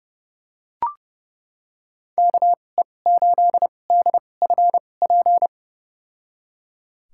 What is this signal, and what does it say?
Morse code, heard as a keyed tone, about 31 wpm, spaced as for 22 wpm, 700 Hz. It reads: KE8DFP
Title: USA Call Signs - 6 Characters - 2x Character Spacing - 30wpm